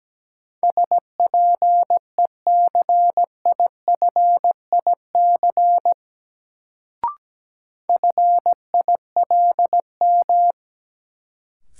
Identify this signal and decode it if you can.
Morse code, heard as a keyed tone, about 17 wpm, 700 Hz. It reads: SPECIFIC FILM
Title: Sentences from Top 1000 Words - Review Part 3 - New Word Drill - 17wpm